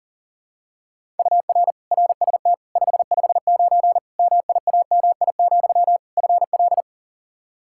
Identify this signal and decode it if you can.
Morse code, heard as a keyed tone, about 37 wpm, 700 Hz. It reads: UR RST 559 MIAMI, FL
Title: K1USN SST QSO Practice 40wpm